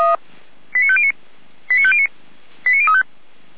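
Five-tone selective call (ZVEI), heard as quick runs of five tones, tones of about 70 ms. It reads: 89509 89509 89035
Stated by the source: ZVEI five-tone selective call 89191 received by SDR.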